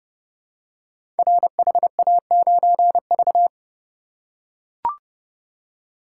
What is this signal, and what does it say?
Morse code, heard as a keyed tone, about 30 wpm, 700 Hz. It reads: RHA9V
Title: Code Groups - 5 Letter:Numbers - 30wpm